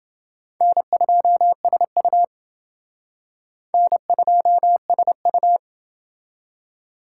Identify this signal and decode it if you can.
Morse code, heard as a keyed tone, about 29 wpm, 700 Hz. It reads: N2SU N2SU